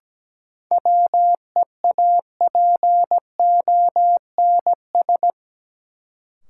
Morse code, heard as a keyed tone, about 17 wpm, 700 Hz. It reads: WEAPONS
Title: Sets of 5 Words Encore - New Word Drill - 17wpm